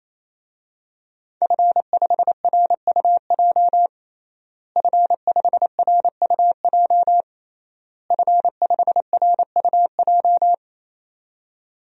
Morse code, heard as a keyed tone, about 28 wpm, 700 Hz. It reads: F5RUJ F5RUJ F5RUJ
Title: DXCC-100 Call Signs - 5 Characters - Repeated 3x - 28wpm